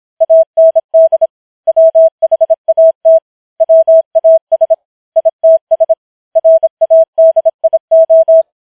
Morse code, heard as a keyed tone, about 26 wpm, 650 Hz. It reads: AND WHAT WAS ITS RADIO
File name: challenge191